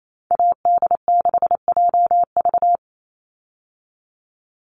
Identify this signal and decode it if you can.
Morse code, heard as a keyed tone, about 28 wpm, 700 Hz. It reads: AD6JV
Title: Call Signs Repeated 3x - 28wpm